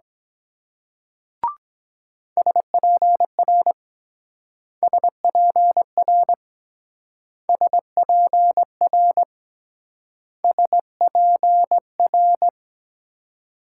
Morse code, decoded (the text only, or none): SPR SPR SPR SPR